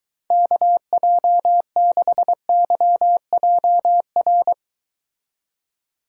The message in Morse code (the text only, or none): KJ6YJR